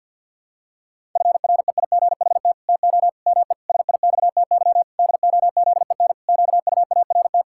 Morse code, heard as a keyed tone, about 52 wpm, 700 Hz. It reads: FREIGHT TO ME SIXTY DOZEN QUART
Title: Pangram Challenge v2 - 50wpm